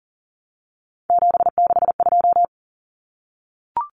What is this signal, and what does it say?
Morse code, heard as a keyed tone, about 40 wpm, 700 Hz. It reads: 762
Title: Common Cut Numbers - 3-Digit 40wpm